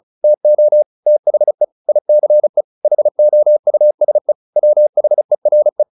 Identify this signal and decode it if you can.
Morse code, heard as a keyed tone, about 34 wpm, 600 Hz. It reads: TO THE ICE HOUSE WHERE
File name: the old man and the sea / the old man and the sea 1 35wpm